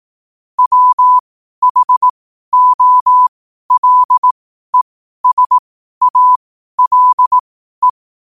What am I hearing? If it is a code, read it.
Morse code, heard as a keyed tone, about 18 wpm, spaced as for 12 wpm, 1000 Hz. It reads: WHOLESALE